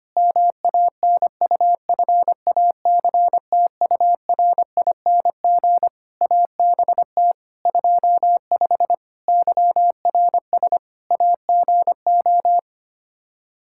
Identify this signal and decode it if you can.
Morse code, heard as a keyed tone, about 25 wpm, 700 Hz. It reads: MANUFACTURING ABT 25 YRS AGO
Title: K1USN SST QSO Practice 25wpm